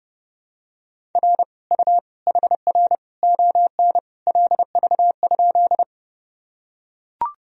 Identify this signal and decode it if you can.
Morse code, heard as a keyed tone, about 30 wpm, 700 Hz. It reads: R U HR ON LV?
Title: Sentences from Top 400 Words and Abbreviations 30wpm